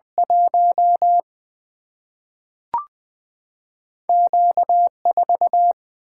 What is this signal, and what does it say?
Morse code, heard as a keyed tone, about 20 wpm, 700 Hz. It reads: 1 Q 4
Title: Two Letter-Number Combinations - 20wpm